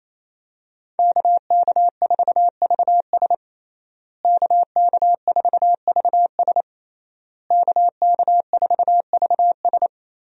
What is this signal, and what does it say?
Morse code, heard as a keyed tone, about 28 wpm, 700 Hz. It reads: KK4VS KK4VS KK4VS
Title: USA Call Signs - 5 Characters - Repeated 3x - 28wpm